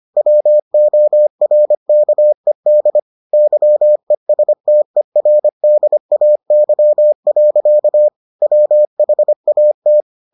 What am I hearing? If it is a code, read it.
Morse code, heard as a keyed tone, about 25 wpm, 600 Hz. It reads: WORKED YESTERDAY. WHAT